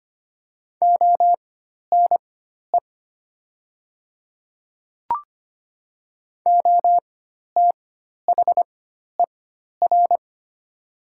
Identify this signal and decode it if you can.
Morse code, heard as a keyed tone, about 25 wpm, spaced as for 12 wpm, 690 Hz. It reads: ONE OTHER
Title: 4x Character Spacing - Top 100 Words - 25wpm